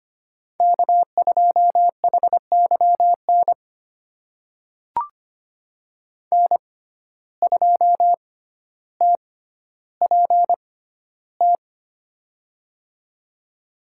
Morse code, heard as a keyed tone, about 25 wpm, spaced as for 13 wpm, 700 Hz. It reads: K2HYN N2TPT
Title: USA Call Signs - 5 Characters - 6x Character Spacing - 25wpm